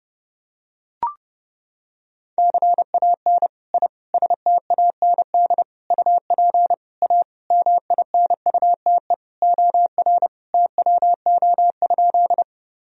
Morse code, heard as a keyed tone, about 32 wpm, 700 Hz. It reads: CAN I STAND UP A MINUTE OR TWO?
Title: Sentences from Top 1000 Words - Review Part 1 - 30wpm